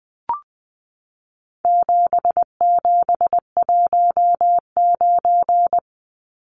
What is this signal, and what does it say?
Morse code, heard as a keyed tone, about 20 wpm, 700 Hz. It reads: 7719
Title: Numbers - 4-Digit 20wpm